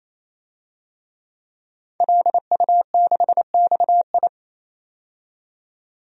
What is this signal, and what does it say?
Morse code, heard as a keyed tone, about 28 wpm, 700 Hz. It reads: LU6XI
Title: DXCC-100 Call Signs - 5 Characters - 28wpm